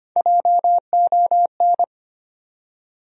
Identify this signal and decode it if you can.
Morse code, heard as a keyed tone, about 25 wpm, 700 Hz. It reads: JON
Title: USA Common Names - Max 3 Letters - 25wpm